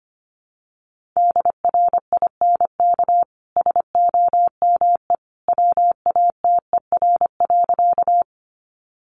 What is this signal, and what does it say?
Morse code, heard as a keyed tone, about 25 wpm, 700 Hz. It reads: DRINK SOME WATER.
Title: Sentences from Top 1500 Words 25wpm